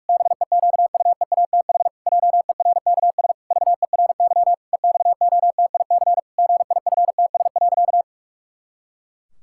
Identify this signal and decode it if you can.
Morse code, heard as a keyed tone, about 46 wpm, 700 Hz. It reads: BEQUEATH JERKS VERY EXOTIC GIFTS.
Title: Pangram Challenge v2 - 45wpm